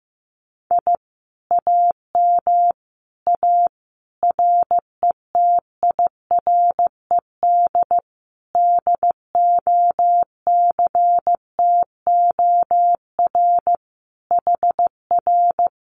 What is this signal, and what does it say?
Morse code, heard as a keyed tone, about 15 wpm, 700 Hz. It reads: I AM A RETIRED DOCTOR HR
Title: Common QSO Phrases Encore 15wpm